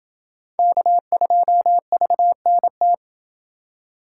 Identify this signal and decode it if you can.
Morse code, heard as a keyed tone, about 29 wpm, 700 Hz. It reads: K2VNT